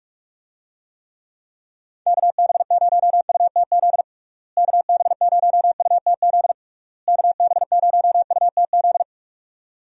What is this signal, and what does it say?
Morse code, heard as a keyed tone, about 46 wpm, 700 Hz. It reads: KB0UTZ KB0UTZ KB0UTZ
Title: USA Call Signs - 6 Characters - Repeated 3x - 45wpm